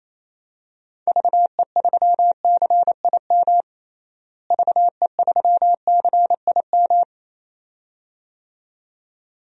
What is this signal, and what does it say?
Morse code, heard as a keyed tone, about 28 wpm, 700 Hz. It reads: VE3CIM VE3CIM
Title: DXCC-100 Call Signs - 6 Characters - Repeated 2x - 28wpm